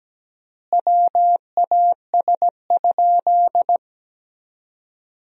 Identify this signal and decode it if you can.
Morse code, heard as a keyed tone, about 17 wpm, 700 Hz. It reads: WAS?
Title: 3x Word-Spacing - Sentences from Top 200 Words - 17wpm